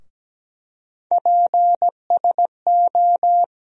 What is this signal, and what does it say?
Morse code, heard as a keyed tone, about 17 wpm, 700 Hz. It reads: PSO
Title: USA Call Sign Postfixes - 3 Letters 17wpm